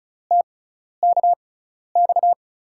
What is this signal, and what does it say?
Morse code, heard as a keyed tone, about 36 wpm, spaced as for 12 wpm, 700 Hz. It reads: TKX
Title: USA Call Signs - 5 Characters - 6x Character Spacing - 35wpm